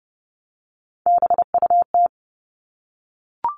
Morse code, heard as a keyed tone, about 30 wpm, 700 Hz. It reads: BUT
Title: Top 300 Words 30wpm